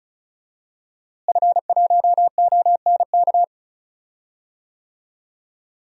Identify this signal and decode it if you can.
Morse code, heard as a keyed tone, about 36 wpm, 700 Hz. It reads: F1ONK
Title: DXCC-100 Call Signs - 5 Characters - Repeated 3x - 35wpm